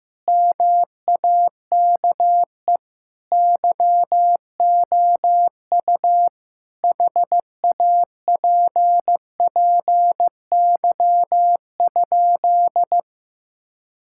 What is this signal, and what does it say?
Morse code, heard as a keyed tone, about 16 wpm, 700 Hz. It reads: MAKE YOU HAPPY?
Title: Sentences from Top 1000 Words - Review Part 6 - 15wpm